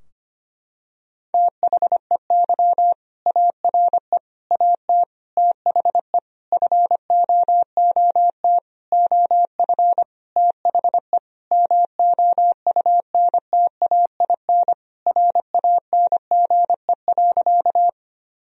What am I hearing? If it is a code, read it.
Morse code, heard as a keyed tone, about 25 wpm, 700 Hz. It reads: THEY ARE AT THE FOOT OF THE MOUNTAIN RANGE.
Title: Sentences from Top 1000 Words - Review Part 4 - 25wpm